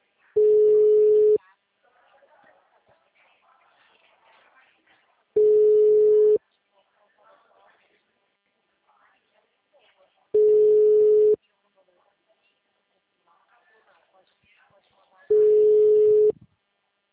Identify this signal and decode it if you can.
Ringback tone (Europe, 425 Hz), with speech.
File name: out- -2017- - - .45866